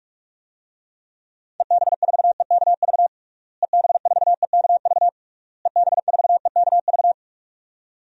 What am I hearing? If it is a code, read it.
Morse code, heard as a keyed tone, about 41 wpm, 700 Hz. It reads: EB4EKV EB4EKV EB4EKV
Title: DXCC-100 Call Signs - 6 Characters - Repeated 3x - 45wpm